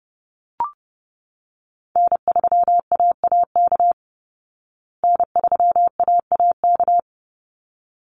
Morse code, heard as a keyed tone, about 30 wpm, 700 Hz. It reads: N3AAK N3AAK
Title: Call Signs Repeated 3x - 30wpm